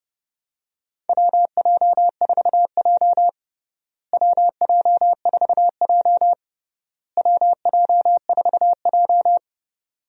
Morse code, heard as a keyed tone, about 30 wpm, 700 Hz. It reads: WJ4J WJ4J WJ4J
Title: USA Call Signs - 4 Characters - Repeated 3x - 30wpm